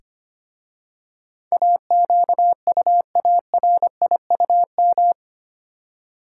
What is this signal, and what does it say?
Morse code, heard as a keyed tone, about 25 wpm, 700 Hz. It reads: AQUARIUM
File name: practice / 25wpm / Easy Words with Hints - 25wpm